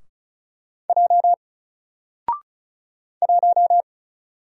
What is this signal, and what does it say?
Morse code, heard as a keyed tone, about 36 wpm, 700 Hz. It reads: J 1
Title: Single Letter-Number Alt 35wpm